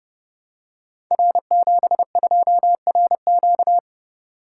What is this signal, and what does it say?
Morse code, heard as a keyed tone, about 30 wpm, 700 Hz. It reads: R72RQ
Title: Code Groups - 5 Letter:Numbers - 30wpm